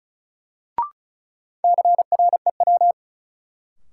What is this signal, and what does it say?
Morse code, heard as a keyed tone, about 36 wpm, 700 Hz. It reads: CREW
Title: Sentences from Top 1500 Words - New Word Drill Alt - 35wpm